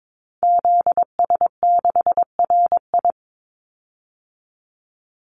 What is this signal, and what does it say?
Morse code, heard as a keyed tone, about 22 wpm, 700 Hz. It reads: ZS6RI
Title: Call Signs Repeated 3x - 22wpm